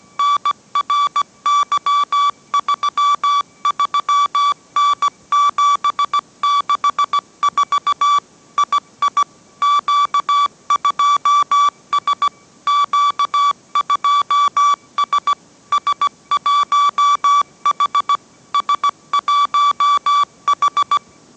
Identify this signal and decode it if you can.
Morse code, heard as a keyed tone, about 17 wpm, 1150 Hz. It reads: NRY33N764 II Q2S Q2S S1H S1H